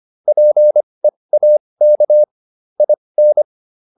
Morse code, heard as a keyed tone, about 25 wpm, spaced as for 20 wpm, 600 Hz. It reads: PEAK IN